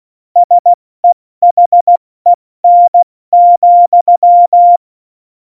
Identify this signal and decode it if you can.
Morse code, heard as a keyed tone, about 16 wpm, 700 Hz. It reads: SEHEN,
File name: Morsekurs / Witze / Witz 03. Oktober 17, Tempo 70 F80